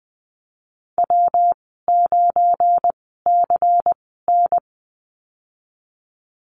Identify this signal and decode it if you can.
Morse code, heard as a keyed tone, about 20 wpm, spaced as for 15 wpm, 700 Hz. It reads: W9CN
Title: 193 Lesson 31g - Letter v - Call Signs 20wpm